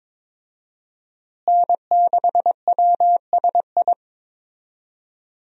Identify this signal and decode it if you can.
Morse code, heard as a keyed tone, about 22 wpm, 700 Hz. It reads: N6WSI
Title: USA Call Signs - 5 Characters - Repeated 2x - 22wpm